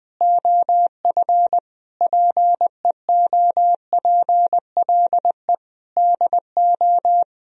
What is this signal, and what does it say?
Morse code, heard as a keyed tone, about 20 wpm, 700 Hz. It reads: OF PEOPLE DO
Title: Speed-Racing - Sets of 5 Words 20wpm